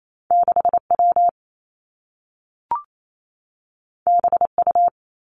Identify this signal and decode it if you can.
Morse code, heard as a keyed tone, about 28 wpm, 700 Hz. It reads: 6 W B U
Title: Two Challenging Letter-Number Combinations 28wpm